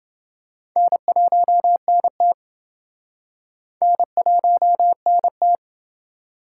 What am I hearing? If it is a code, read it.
Morse code, heard as a keyed tone, about 30 wpm, 700 Hz. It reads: N1NT N1NT